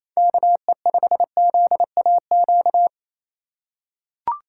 Morse code, heard as a keyed tone, about 28 wpm, 700 Hz. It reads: KE5ZAQ
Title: USA Call Signs - 6 Characters - 2x Character Spacing - 28wpm